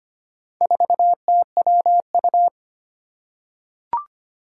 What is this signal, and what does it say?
Morse code, heard as a keyed tone, about 25 wpm, 700 Hz. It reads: 4TWU
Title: Four Challenging Letter-Number Combinations 25wpm